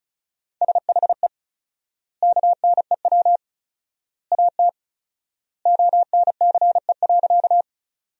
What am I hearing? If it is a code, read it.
Morse code, heard as a keyed tone, about 34 wpm, 700 Hz. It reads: SHE KNEW AT ONCE.
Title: 4x Word-Spacing - Sentences from Top 400 Words - 35wpm